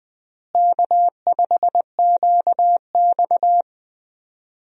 Morse code, heard as a keyed tone, about 21 wpm, 700 Hz. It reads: K5QX
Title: USA Call Signs - 4 Characters - Repeated 3x - 20wpm